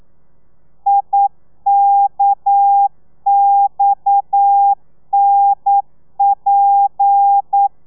Morse code, heard as a keyed tone, about 9 wpm, 800 Hz. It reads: IKXNP
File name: telegrafia / i2tfj / Lez7 40